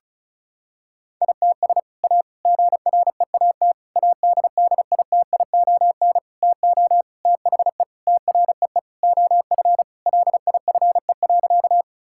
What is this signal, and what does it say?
Morse code, heard as a keyed tone, about 34 wpm, 700 Hz. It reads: ITS A GREAT ADDITION TO THE TREE OF LIFE.
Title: Sentences from Top 1000 Words - Review Part 6 - 35wpm